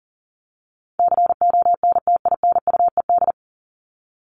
Morse code, heard as a keyed tone, about 40 wpm, 700 Hz. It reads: CONTINUED
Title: ED Suffix - Top 100 Words with Base Words 40wpm